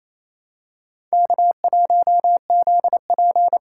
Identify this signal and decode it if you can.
Morse code, heard as a keyed tone, about 28 wpm, 700 Hz. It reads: K1ZP
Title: USA Call Signs - 4 Characters - 4x Character Spacing - 28wpm